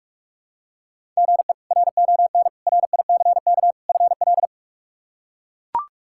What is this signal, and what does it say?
Morse code, heard as a keyed tone, about 41 wpm, 700 Hz. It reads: GE RON RICK FL
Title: K1USN SST QSO Practice - 45wpm